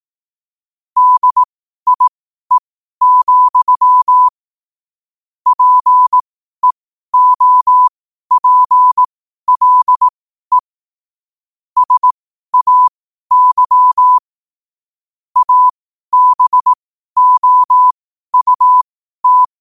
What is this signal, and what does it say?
Morse code, heard as a keyed tone, about 18 wpm, spaced as for 12 wpm, 1000 Hz. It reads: DIE, PEOPLE SAY ABOUT
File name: qotdfiles / 12wpm